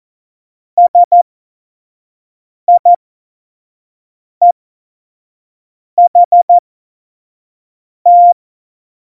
Morse code, heard as a keyed tone, about 14 wpm, spaced as for 5 wpm, 700 Hz. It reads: SIEHT
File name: Morsekurs / Witze / Witz 28. Januar 2020, 25 F70